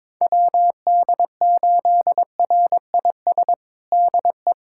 Morse code, heard as a keyed tone, about 22 wpm, 700 Hz. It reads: WD8RIS DE
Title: USA Callsign DE Callsign - 6 Characters - 22wpm